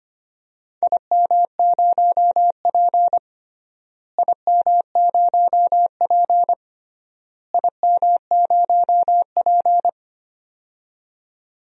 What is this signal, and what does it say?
Morse code, heard as a keyed tone, about 25 wpm, 700 Hz. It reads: IM0P IM0P IM0P
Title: DXCC-100 Call Signs - 4 Characters - Repeated 3x - 25wpm